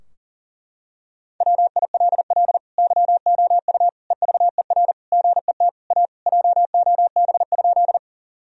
Morse code, heard as a keyed tone, about 40 wpm, 700 Hz. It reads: WILL YOU EVER GET A JOB?
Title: Sentences from Top 1000 Words - Review Part 3 - 40wpm